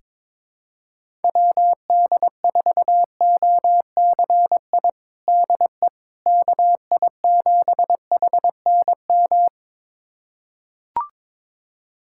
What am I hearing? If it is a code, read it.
Morse code, heard as a keyed tone, about 22 wpm, 700 Hz. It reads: WD4OCI DE KI7HNM
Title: USA Callsign DE Callsign - 6 Characters - 22wpm